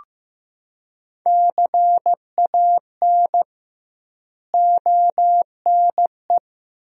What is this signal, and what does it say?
Morse code, heard as a keyed tone, about 15 wpm, 700 Hz. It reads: CAN ONE
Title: 2x Word-Spacing - Sentences from Top 100 Words - 15wpm